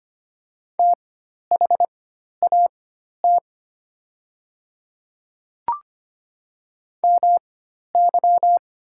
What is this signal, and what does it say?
Morse code, heard as a keyed tone, about 25 wpm, spaced as for 12 wpm, 700 Hz. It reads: THAT MY
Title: 4x Character Spacing - Top 100 Words - 25wpm